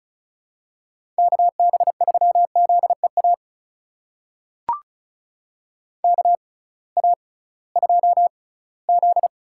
Morse code, heard as a keyed tone, about 34 wpm, spaced as for 20 wpm, 700 Hz. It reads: KB3ZEA KA2Z
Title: USA Call Signs - 6 Characters - 6x Character Spacing - 35wpm